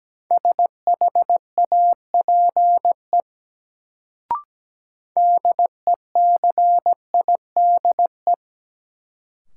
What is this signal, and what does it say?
Morse code, heard as a keyed tone, about 17 wpm, 700 Hz. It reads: SHAPE DECIDE
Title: Sentences from Top 500 Words - New Word Drill - 17wpm